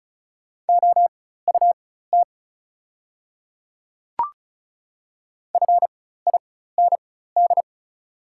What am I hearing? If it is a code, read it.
Morse code, heard as a keyed tone, about 34 wpm, spaced as for 16 wpm, 700 Hz. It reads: OUT FIND
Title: 4x Character Spacing - Top 100 Words - 35wpm